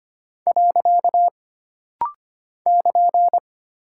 Morse code, heard as a keyed tone, about 25 wpm, 700 Hz. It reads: . <KN>
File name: Single Punctuation-Prosign - Rapid-Fire - 25wpm